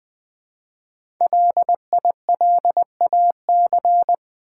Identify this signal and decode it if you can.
Morse code, heard as a keyed tone, about 20 wpm, 700 Hz. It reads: LILAC
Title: Easy Words with Hints - 20wpm